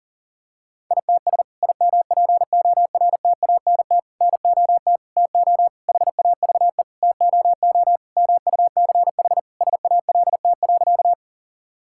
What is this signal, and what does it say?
Morse code, heard as a keyed tone, about 40 wpm, 700 Hz. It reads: ITS IMPORTANT NOT TO HAVE TOO MUCH SALT.
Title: Sentences from Top 1000 Words - Review Part 5 - 40wpm